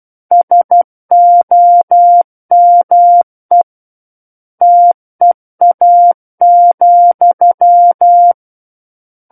Morse code, heard as a keyed tone, about 12 wpm, 700 Hz. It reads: SOME TEA,
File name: Book One / 03-Chapter 3 - 12 WPM